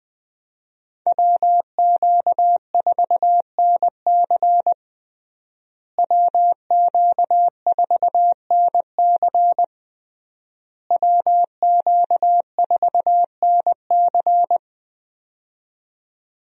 Morse code, heard as a keyed tone, about 20 wpm, 700 Hz. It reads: WQ4NC WQ4NC WQ4NC
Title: USA Call Signs - 5 Characters - Repeated 3x - 20wpm